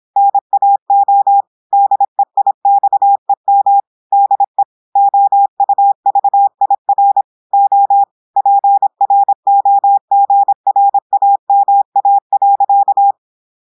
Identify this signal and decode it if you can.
Morse code, heard as a keyed tone, about 26 wpm, 800 Hz. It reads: NAO DEIXEM DE OUVIR O PROGRAMA.
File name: portugais texte en clair 2 vitesse 26 mots minutes